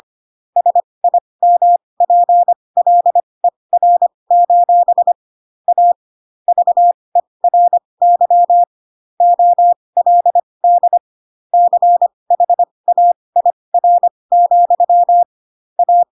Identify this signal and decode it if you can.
Morse code, heard as a keyed tone, about 25 wpm, spaced as for 20 wpm, 700 Hz. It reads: SIMPLER: A VERY OLD CHAIR, A